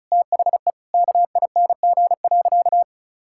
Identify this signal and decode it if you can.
Morse code, heard as a keyed tone, about 36 wpm, 700 Hz. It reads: THE KING.
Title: Sentences from Top 1000 Words - Review Part 6 - 35wpm